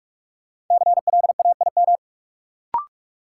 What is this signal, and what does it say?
Morse code, heard as a keyed tone, about 46 wpm, 700 Hz. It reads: CLAIM
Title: Sentences from Top 1500 Words - New Word Drill - 45wpm